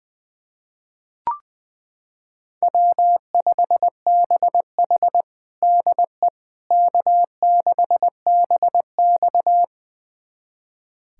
Morse code, heard as a keyed tone, about 20 wpm, 700 Hz. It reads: W5BH DE K6BX
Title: USA Callsign DE Callsign - 4 Characters - 20wpm